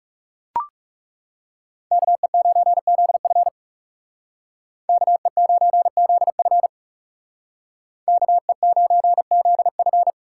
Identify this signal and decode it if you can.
Morse code, heard as a keyed tone, about 38 wpm, 700 Hz. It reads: KE9ZF KE9ZF KE9ZF